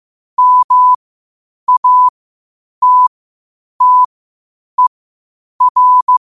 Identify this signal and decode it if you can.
Morse code, heard as a keyed tone, about 15 wpm, spaced as for 8 wpm, 1000 Hz. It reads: MATTER
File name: qotdfiles / 8wpm